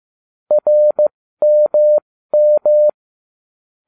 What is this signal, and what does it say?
Morse code, heard as a keyed tone, about 15 wpm, spaced as for 13 wpm, 600 Hz. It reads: RMM